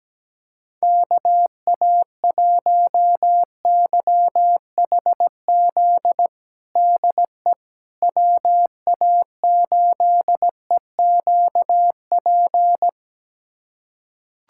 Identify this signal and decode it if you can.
Morse code, heard as a keyed tone, about 17 wpm, 700 Hz. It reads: KA1YHZ DE WA8EQP
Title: USA Callsign DE Callsign - 6 Characters - 17wpm